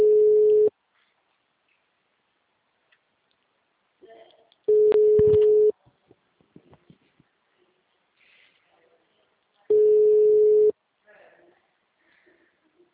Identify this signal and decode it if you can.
Ringback tone (Europe, 425 Hz), with speech.
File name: out- -2032- - - .15756